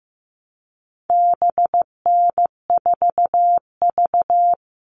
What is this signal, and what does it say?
Morse code, heard as a keyed tone, about 15 wpm, 700 Hz. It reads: BN4V
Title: All Cut Numbers - 4-Digit 15wpm